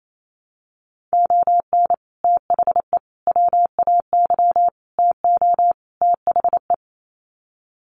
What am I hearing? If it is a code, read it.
Morse code, heard as a keyed tone, about 28 wpm, 700 Hz. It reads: ON THE WAY TO THE
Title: Sets of 5 Words 28wpm